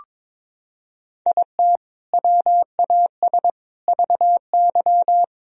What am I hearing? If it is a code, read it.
Morse code, heard as a keyed tone, about 22 wpm, 700 Hz. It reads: IT WAS VY
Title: Sentences from Top 500 Words and Abbreviations 22wpm